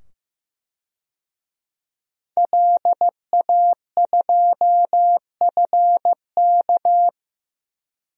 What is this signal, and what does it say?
Morse code, heard as a keyed tone, about 15 wpm, 700 Hz. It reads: LA2FK
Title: DXCC-100 Call Signs - 5 Characters - 15wpm